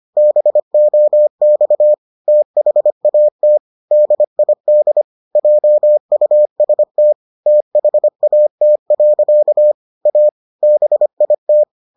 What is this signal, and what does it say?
Morse code, heard as a keyed tone, about 25 wpm, 600 Hz. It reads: BOX THAT DID JUST THAT. A BIT